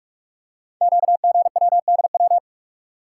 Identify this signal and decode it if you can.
Morse code, heard as a keyed tone, about 44 wpm, 700 Hz. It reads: QGWDW
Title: Code Groups - 5 Letters - 45wpm